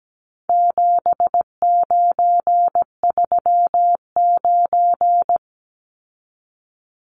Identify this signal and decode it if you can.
Morse code, heard as a keyed tone, about 17 wpm, 700 Hz. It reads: 7939
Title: Numbers - 4-Digit 17wpm